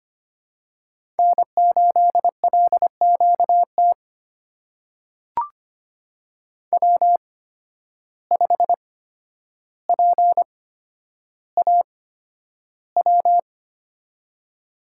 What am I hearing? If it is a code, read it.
Morse code, heard as a keyed tone, about 25 wpm, spaced as for 11 wpm, 700 Hz. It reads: N8LQT W5PAW
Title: USA Call Signs - 8x Character Spacing - 25wpm